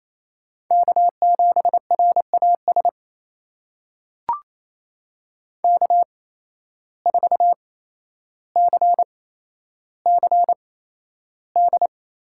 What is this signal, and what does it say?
Morse code, heard as a keyed tone, about 28 wpm, spaced as for 12 wpm, 700 Hz. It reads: K7RAS K4CCD
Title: USA Call Signs - 5 Characters - 8x Character Spacing - 28wpm